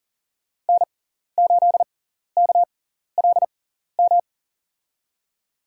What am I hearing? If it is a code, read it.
Morse code, heard as a keyed tone, about 40 wpm, spaced as for 14 wpm, 700 Hz. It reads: N8KLM
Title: USA Call Signs - 6x Character Spacing - 40wpm